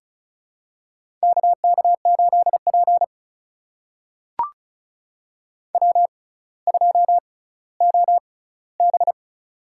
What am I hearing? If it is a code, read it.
Morse code, heard as a keyed tone, about 36 wpm, spaced as for 18 wpm, 700 Hz. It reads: KK8P W2OB
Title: USA Call Signs - 4 Characters - 6x Character Spacing - 35wpm